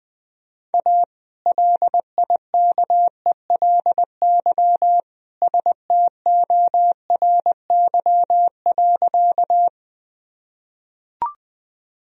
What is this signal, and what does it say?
Morse code, heard as a keyed tone, about 21 wpm, 700 Hz. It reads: A LIKELY STORY.
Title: Sentences from Top 1000 Words - Review Part 4 - 20wpm